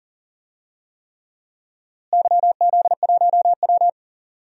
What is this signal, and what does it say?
Morse code, heard as a keyed tone, about 40 wpm, 700 Hz. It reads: YZ1W
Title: DXCC-100 Call Signs - 4 Characters - Repeated 3x - 40wpm